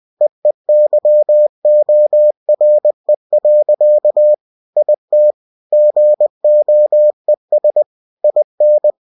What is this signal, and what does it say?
Morse code, heard as a keyed tone, about 20 wpm, 600 Hz. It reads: EEYORE. IT GOES IN
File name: cw-books / winnie-the-pooh / winnie-20wpm-page-0058